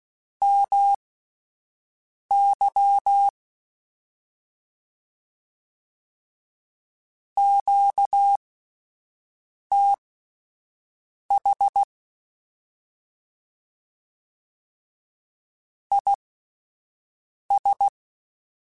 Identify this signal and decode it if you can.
Morse code, heard as a keyed tone, about 16 wpm, spaced as for 5 wpm, 800 Hz. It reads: MY QTH IS
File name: QSOs / QSOs-mp3s / WD0BGP 1@13